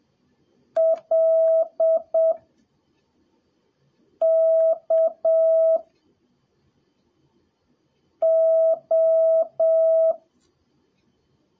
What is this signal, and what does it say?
Morse code, heard as a keyed tone, about 7 wpm, spaced as for 3 wpm, 650 Hz. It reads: LKO
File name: jh8cbh / 2021cw / cw03a